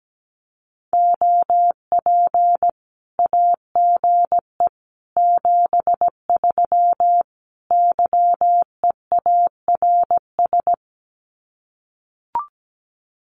Morse code, heard as a keyed tone, about 17 wpm, 700 Hz. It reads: OP AGE 73 YEARS
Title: Common QSO Phrases 17wpm